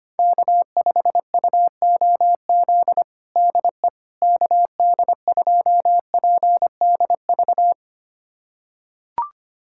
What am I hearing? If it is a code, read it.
Morse code, heard as a keyed tone, about 25 wpm, 700 Hz. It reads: K5UOZ DE KD2PDV
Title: USA Callsign DE Callsign 25wpm